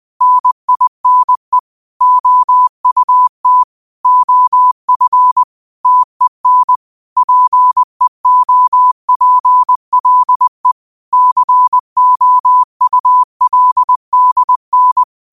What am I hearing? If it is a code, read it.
Morse code, heard as a keyed tone, about 20 wpm, 1000 Hz. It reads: NINE OUT OF TEN PEOPLE COULDN
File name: qotdfiles / 20wpm